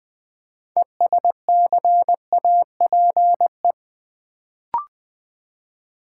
Morse code, heard as a keyed tone, about 20 wpm, 700 Hz. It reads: ESCAPE
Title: Build A Word - 2 Letters At A Time - 6-Letter Words - 20wpm